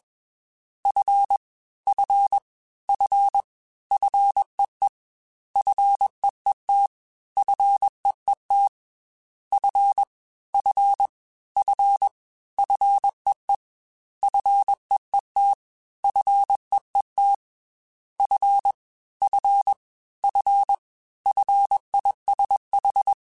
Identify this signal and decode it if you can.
Morse code, heard as a keyed tone, about 21 wpm, 800 Hz. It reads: F F F FEE FEET FEET F F F FEE FEET FEET F F F FISH
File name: Fs / Single F mp3s / Single F Prefixes 3x2 Twice @20